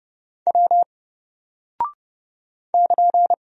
Morse code, heard as a keyed tone, about 30 wpm, 700 Hz. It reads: W <KN>
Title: Single Character (Letter:Number:Punctuation:Prosign) 30wpm